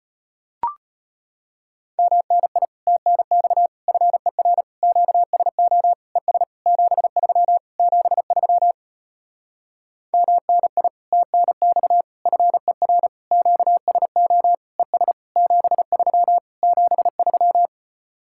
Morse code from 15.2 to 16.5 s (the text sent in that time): 73